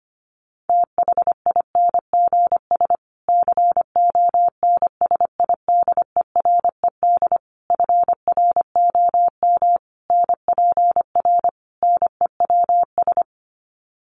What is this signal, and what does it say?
Morse code, heard as a keyed tone, about 25 wpm, 700 Hz. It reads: THINGS CONSIDERED FROM NPR NEWS
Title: Sets of 5 Words Encore 25wpm